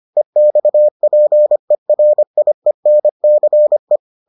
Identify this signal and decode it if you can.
Morse code, heard as a keyed tone, about 25 wpm, 600 Hz. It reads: EXPERIENCE